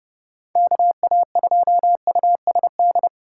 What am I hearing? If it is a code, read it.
Morse code, heard as a keyed tone, about 29 wpm, 700 Hz. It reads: KA2USD